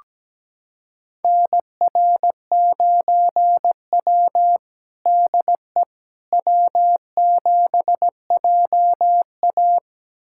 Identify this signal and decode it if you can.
Morse code, heard as a keyed tone, about 17 wpm, 700 Hz. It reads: NR9W DE W7JA
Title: USA Callsign DE Callsign - 4 Characters - 17wpm